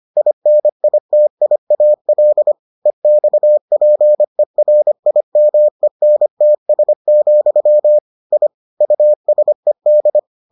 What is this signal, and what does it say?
Morse code, heard as a keyed tone, about 25 wpm, 600 Hz. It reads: INITIAL EXPERIMENTS, I USED